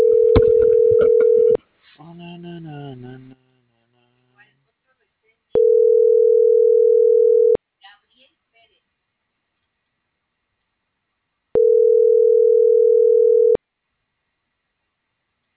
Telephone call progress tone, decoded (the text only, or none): ringback tone